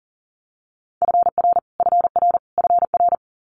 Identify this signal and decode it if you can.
Morse code, heard as a keyed tone, about 40 wpm, 700 Hz. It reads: FR FR FR
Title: Multi-letter Phonemes Repeated 3x 40wpm